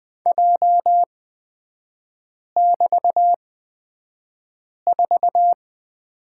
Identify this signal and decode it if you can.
Morse code, heard as a keyed tone, about 20 wpm, spaced as for 5 wpm, 700 Hz. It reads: J<BT>4
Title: 250 Lesson 40e - BK - Three Characters 20wpm